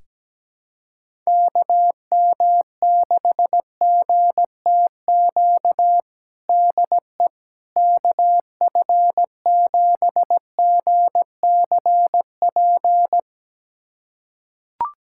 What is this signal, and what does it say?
Morse code, heard as a keyed tone, about 17 wpm, 700 Hz. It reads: KM6GTQ DE KF7GCP
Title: USA Callsign DE Callsign - 6 Characters - 17wpm